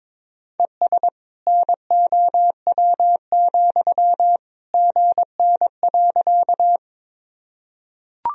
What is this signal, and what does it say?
Morse code, heard as a keyed tone, about 22 wpm, 700 Hz. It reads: ES NOW, GN.
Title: Sentences from Top 300 Words and Abbreviations 22wpm